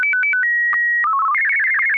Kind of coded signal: SSTV (slow-scan television)